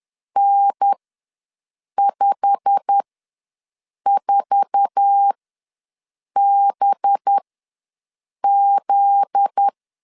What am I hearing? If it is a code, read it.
Morse code, heard as a keyed tone, about 10 wpm, spaced as for 6 wpm, 800 Hz. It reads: N54BZ
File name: LEZIONI / 8 YJ45 / 8 YJ45 ws 10wpm 3843